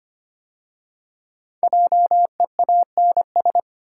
Morse code, heard as a keyed tone, about 25 wpm, 700 Hz. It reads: JEANS
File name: 5 Letter Words - First 3 Letters Given - 25wpm